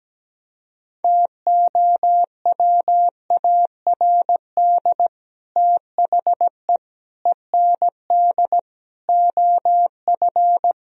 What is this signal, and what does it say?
Morse code, heard as a keyed tone, about 17 wpm, 700 Hz. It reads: TOWARD THE END OF
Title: Speed-Racing - Sets of 5 Words 17wpm